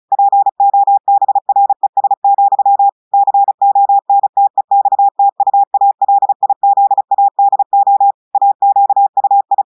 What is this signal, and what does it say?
Morse code, heard as a keyed tone, about 36 wpm, 800 Hz. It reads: POBRES, CONTEXTUALIZADO AQUI
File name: audios / portugais / portugais texte en clair 1 vitesse 35 mots minutes